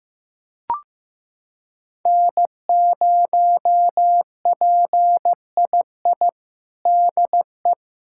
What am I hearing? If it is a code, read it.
Morse code, heard as a keyed tone, about 15 wpm, 700 Hz. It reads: N0PII DE
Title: USA Callsign DE Callsign - 5 Characters - 15wpm